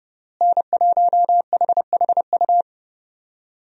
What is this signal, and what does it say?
Morse code, heard as a keyed tone, about 31 wpm, 700 Hz. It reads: N1HHU
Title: USA Call Signs - 5 Characters - Repeated 3x - 30wpm